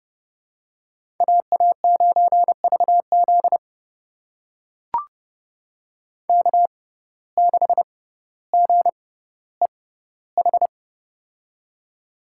Morse code, heard as a keyed tone, about 32 wpm, spaced as for 16 wpm, 700 Hz. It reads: AA9VZ K6GEH
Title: USA Call Signs - 5 Characters - 6x Character Spacing - 30wpm